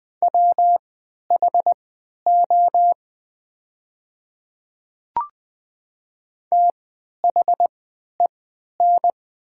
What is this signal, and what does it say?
Morse code, heard as a keyed tone, about 20 wpm, spaced as for 11 wpm, 700 Hz. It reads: WHO THEN
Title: 3x Character Spacing - Top 100 Words - 20wpm